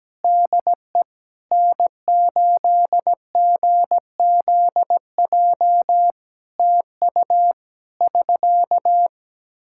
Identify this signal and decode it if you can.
Morse code, heard as a keyed tone, about 17 wpm, 700 Hz. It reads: DE N8GZJ TU <SK>
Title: Practice QSO - Exchange 4 - Ending - 17wpm